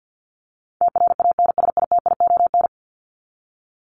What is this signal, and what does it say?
Morse code, heard as a keyed tone, about 49 wpm, 700 Hz. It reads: TRANSITION
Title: Top 1751-2000 Words 50wpm